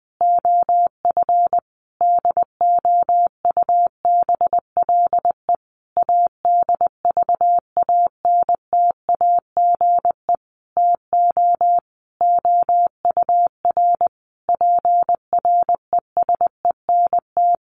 Morse code, heard as a keyed tone, about 20 wpm, 700 Hz. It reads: OF DOUBLE ADVANTAGE TO OUR PRESENT
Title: Sentences from Top 1400 Words 20wpm